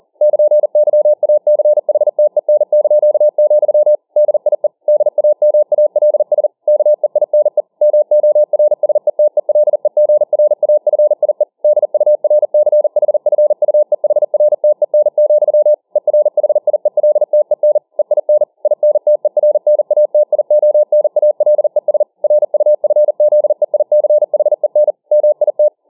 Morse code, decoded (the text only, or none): <KN>YACHTEN), DIE DAMALS KEINE MORSETELEGRAFIE DURCHFUEHRTEN, ERHIELTEN EIN INTERNATIONALES RUFZEICHEN MIT